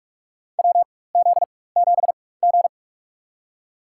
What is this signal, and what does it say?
Morse code, heard as a keyed tone, about 46 wpm, spaced as for 21 wpm, 700 Hz. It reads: WZ7G
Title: USA Call Signs - 4x Character Spacing - 45wpm